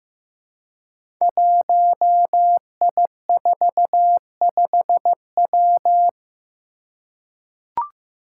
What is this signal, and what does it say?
Morse code, heard as a keyed tone, about 15 wpm, 700 Hz. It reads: 1I45W
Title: Five Challenging Letter-Number Combinations 15wpm